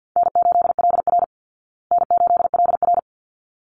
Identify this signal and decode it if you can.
Morse code, heard as a keyed tone, about 50 wpm, 700 Hz. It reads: N8LR N8LR
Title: Easy Call Signs Repeated 2x - 50wpm